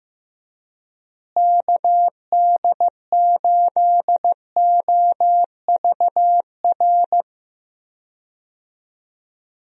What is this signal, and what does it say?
Morse code, heard as a keyed tone, about 15 wpm, 700 Hz. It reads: KD8OVR
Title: USA Call Signs - 6 Characters - Repeated 2x - 15wpm